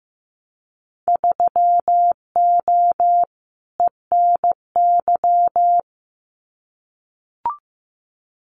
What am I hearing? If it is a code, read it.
Morse code, heard as a keyed tone, about 15 wpm, 700 Hz. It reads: 3O ENY
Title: Winter Field Day Exchanges Encore 15wpm